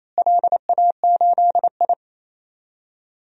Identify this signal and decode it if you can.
Morse code, heard as a keyed tone, about 28 wpm, 700 Hz. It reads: LA8I
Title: DXCC-100 Call Signs - 4 Characters - Repeated 2x - 28wpm